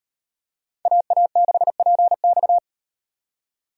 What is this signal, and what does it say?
Morse code, heard as a keyed tone, about 38 wpm, 700 Hz. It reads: AA6PX